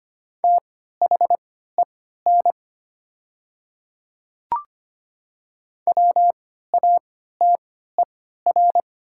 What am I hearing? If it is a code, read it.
Morse code, heard as a keyed tone, about 25 wpm, spaced as for 14 wpm, 700 Hz. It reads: THEN WATER
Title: 3x Character Spacing - Top 100 Words - 25wpm